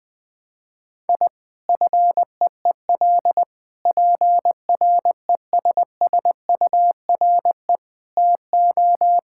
Morse code, heard as a keyed tone, about 21 wpm, 700 Hz. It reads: I FEEL PRESSURE TO
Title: Sentences from Top 1000 Words - Review Part 6 - 20wpm